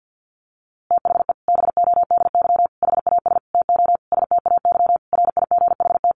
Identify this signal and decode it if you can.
Morse code, heard as a keyed tone, about 50 wpm, 700 Hz. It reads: THE BODY HAS TO STAY RIGHT
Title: A Sound Of Thunder 50wpm